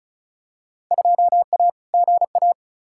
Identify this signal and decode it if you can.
Morse code, heard as a keyed tone, about 36 wpm, 700 Hz. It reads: 2A GA
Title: Field Day Exchanges Encore - 35wpm